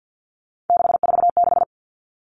Morse code, heard as a keyed tone, about 50 wpm, 700 Hz. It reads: 646
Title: Three Challenging Letter-Number Combinations 50wpm